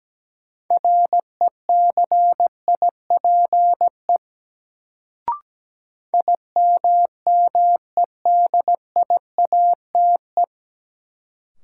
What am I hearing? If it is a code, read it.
Morse code, heard as a keyed tone, about 17 wpm, 700 Hz. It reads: RECIPE IMMEDIATE
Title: Sentences from Top 1200 Words - New Word Drill - 17wpm